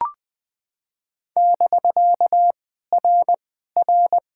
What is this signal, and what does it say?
Morse code, heard as a keyed tone, about 20 wpm, 700 Hz. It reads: <BK> R R
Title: POTA QSO Practice - 20wpm